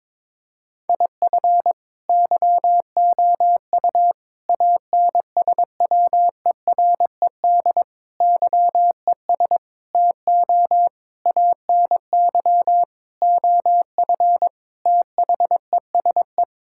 Morse code, heard as a keyed tone, about 22 wpm, 700 Hz. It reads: IF YOU ANSWERED YES TO ANY OF THESE
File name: Sets of 9 Words - 22wpm